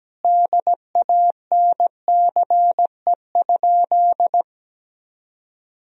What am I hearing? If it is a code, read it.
Morse code, heard as a keyed tone, about 17 wpm, 700 Hz. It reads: DANCE?
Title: Sentences from Top 1000 Words - Review Part 4 - 17wpm